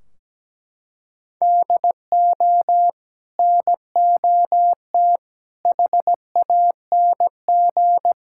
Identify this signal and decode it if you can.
Morse code, heard as a keyed tone, about 17 wpm, 700 Hz. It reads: DO NOT HANG
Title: Sentences from Top 1600 Words 17wpm